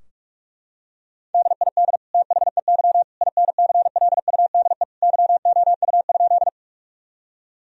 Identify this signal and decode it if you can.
Morse code, heard as a keyed tone, about 45 wpm, 700 Hz. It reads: DID THEY INCLUDE YOU?
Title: Sentences from Top 1000 Words - Review Part 3 - 45wpm